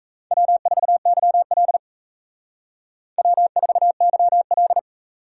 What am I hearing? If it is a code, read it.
Morse code, heard as a keyed tone, about 35 wpm, 700 Hz. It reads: W4YL W4YL